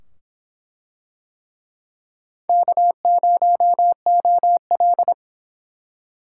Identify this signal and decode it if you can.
Morse code, heard as a keyed tone, about 24 wpm, 700 Hz. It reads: K0OL